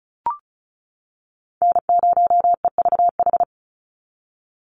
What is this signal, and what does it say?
Morse code, heard as a keyed tone, about 34 wpm, 700 Hz. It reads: N0EVH
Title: NA SOTA Chaser Callsigns 35wpm